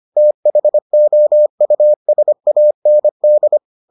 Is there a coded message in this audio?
Yes: Morse code from 0.2 to 3.6 s, heard as a keyed tone, about 25 wpm, 600 Hz. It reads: THOUSAND